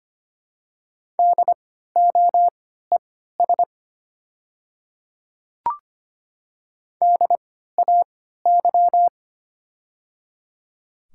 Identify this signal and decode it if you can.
Morse code, heard as a keyed tone, about 25 wpm, spaced as for 14 wpm, 700 Hz. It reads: DOES DAY
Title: 3x Character Spacing - Top 300 Words - 25wpm